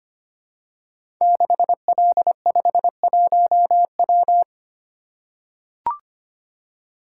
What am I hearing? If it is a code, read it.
Morse code, heard as a keyed tone, about 25 wpm, 700 Hz. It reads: 6L51W
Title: Code Groups - 5 Letter:Numbers - 25wpm